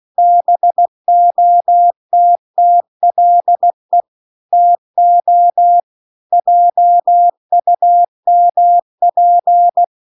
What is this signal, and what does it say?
Morse code, heard as a keyed tone, about 16 wpm, 700 Hz. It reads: BOTTLE TO JUMP
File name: CW-The-Big-Twoheaded-River-2-16WPM